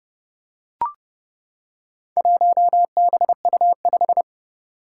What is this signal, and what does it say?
Morse code, heard as a keyed tone, about 30 wpm, 700 Hz. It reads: 1BU5
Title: Four Challenging Letter-Number Combinations 30wpm